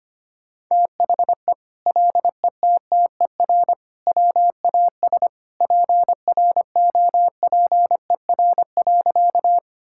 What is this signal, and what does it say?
Morse code, heard as a keyed tone, about 25 wpm, 700 Hz. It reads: THE LETTER WAS PROPER.
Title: Sentences from Top 1000 Words - Review Part 5 - 25wpm